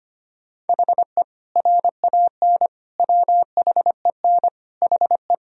Morse code, heard as a keyed tone, about 25 wpm, 700 Hz. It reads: HE RAN WHEN HE
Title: Sentences from Top 1000 Words - Review Part 3 - 25wpm